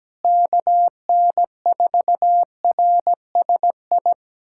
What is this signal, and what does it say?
Morse code, heard as a keyed tone, about 17 wpm, 700 Hz. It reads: KN4RSI
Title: USA Callsign DE Callsign - 6 Characters - 17wpm